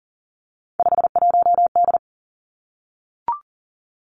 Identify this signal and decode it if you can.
Morse code, heard as a keyed tone, about 38 wpm, 700 Hz. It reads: 51D
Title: Three Challenging Letter-Number Combinations 40wpm